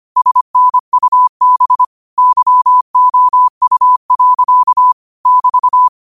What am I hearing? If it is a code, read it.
Morse code, heard as a keyed tone, about 25 wpm, 1000 Hz. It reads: SNUB YOU. <BT>